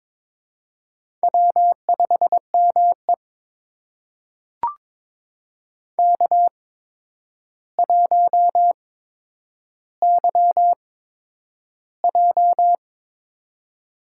Morse code, heard as a keyed tone, about 22 wpm, spaced as for 10 wpm, 700 Hz. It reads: W5ME K1YJ
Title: USA Call Signs - 4 Characters - 8x Character Spacing - 22wpm